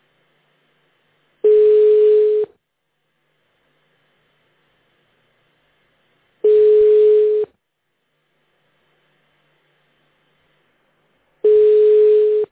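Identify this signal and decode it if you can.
Ringback tone (Europe, 425 Hz).
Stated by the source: Phone call recorded on the petrosvyaz mdpbx PBX system.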